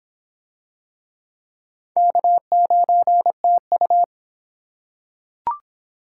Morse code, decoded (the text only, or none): K9TU